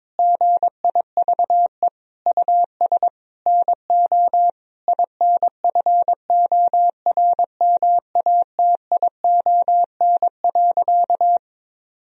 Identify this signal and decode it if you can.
Morse code, heard as a keyed tone, about 22 wpm, 700 Hz. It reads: GIVE US NO INFORMATION.
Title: Sentences from Top 1000 Words - Review Part 3 - 22wpm